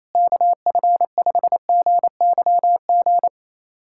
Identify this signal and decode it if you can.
Morse code, heard as a keyed tone, about 28 wpm, 700 Hz. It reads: KF5GYG
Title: USA Callsign DE Callsign 28wpm